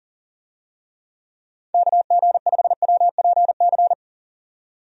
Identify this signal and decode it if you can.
Morse code, heard as a keyed tone, about 40 wpm, 700 Hz. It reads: KG5WPC
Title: USA Call Signs - 6 Characters - 40wpm